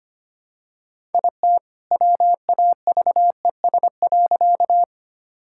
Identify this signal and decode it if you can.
Morse code, heard as a keyed tone, about 25 wpm, 700 Hz. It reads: IT WAVES.
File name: practice / 25wpm / All-Age Appropriate Jokes - 25wpm